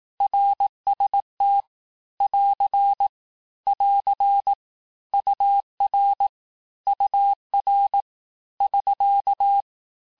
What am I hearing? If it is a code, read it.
Morse code, heard as a keyed tone, about 18 wpm, 800 Hz. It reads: RST <AR> <AR> UR UR <SK>
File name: cours-du-lundi files / cours-1588-texte-abreviations-vitesse-17mpm-puis-22mpm